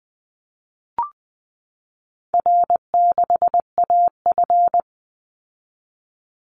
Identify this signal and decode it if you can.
Morse code, heard as a keyed tone, about 20 wpm, 700 Hz. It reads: R6AF
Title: Easy Call Signs Repeated 3x - 20wpm